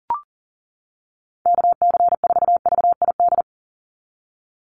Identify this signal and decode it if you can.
Morse code, heard as a keyed tone, about 40 wpm, 700 Hz. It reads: KC4VID
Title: USA Call Signs 40wpm